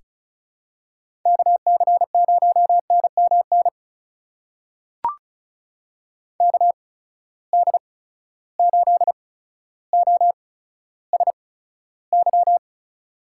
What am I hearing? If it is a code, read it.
Morse code, heard as a keyed tone, about 36 wpm, spaced as for 15 wpm, 700 Hz. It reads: KC0NMN KD8OSY
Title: USA Call Signs - 6 Characters - 8x Character Spacing - 35wpm